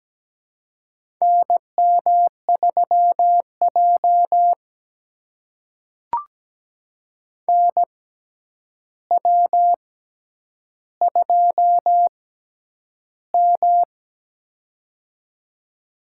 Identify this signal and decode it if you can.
Morse code, heard as a keyed tone, about 17 wpm, spaced as for 9 wpm, 700 Hz. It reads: NM3J NW2M
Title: USA Call Signs - 4 Characters - 6x Character Spacing - 17wpm